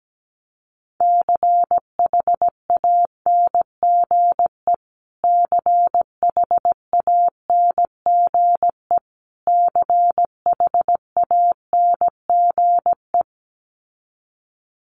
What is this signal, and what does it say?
Morse code, heard as a keyed tone, about 17 wpm, 700 Hz. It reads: CHANGE CHANGE CHANGE
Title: Top 300 Words Repeated 17wpm